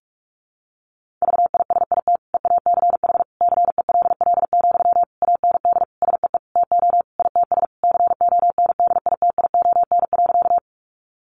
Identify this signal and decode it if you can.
Morse code, heard as a keyed tone, about 44 wpm, 700 Hz. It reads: VISIT EACH CELL, AND SEE TO ITS CONDITION.
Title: Sentences from Top 1100 Words 45wpm